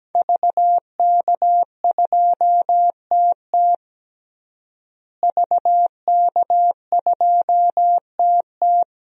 Morse code, heard as a keyed tone, about 17 wpm, 700 Hz. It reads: VK2TT VK2TT
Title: DXCC-100 Call Signs - 5 Characters - Repeated 3x - 17wpm